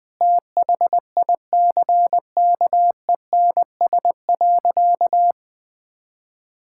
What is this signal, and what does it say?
Morse code, heard as a keyed tone, about 20 wpm, 700 Hz. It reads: THICKENS.
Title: Sentences with 5-Word Phrases - 20wpm